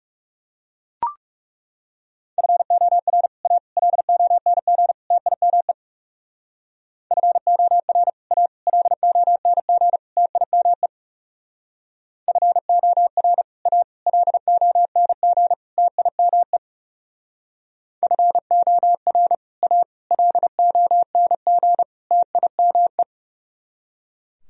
Morse code, heard as a keyed tone, about 38 wpm, 700 Hz. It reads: FOR A LONG TIME FOR A LONG TIME FOR A LONG TIME FOR A LONG TIME